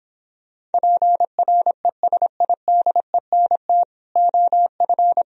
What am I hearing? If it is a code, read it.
Morse code, heard as a keyed tone, about 29 wpm, 700 Hz. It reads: PRESIDENT OF